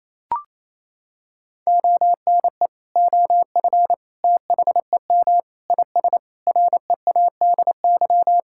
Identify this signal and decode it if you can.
Morse code, heard as a keyed tone, about 28 wpm, 700 Hz. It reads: ONE OF THEM IS READY
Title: Sentences from Top 1600 Words 28wpm